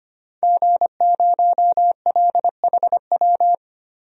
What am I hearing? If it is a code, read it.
Morse code, heard as a keyed tone, about 25 wpm, 700 Hz. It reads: G0LHW
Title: DXCC-100 Call Signs - 5 Characters - 25wpm